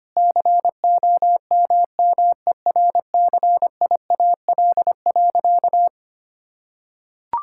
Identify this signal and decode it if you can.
Morse code, heard as a keyed tone, about 25 wpm, 700 Hz. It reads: COMMERCIAL.
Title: Sentences from Top 1000 Words - Review Part 5 - 25wpm